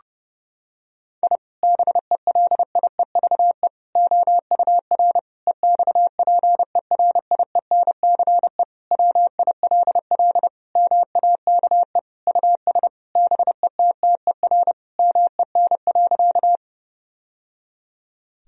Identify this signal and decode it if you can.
Morse code, heard as a keyed tone, about 30 wpm, 700 Hz. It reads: I BELIEVE OUR EXPERIENCE WILL MAKE US BETTER MEN.
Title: Sentences from Top 1000 Words - Review Part 3 - 30wpm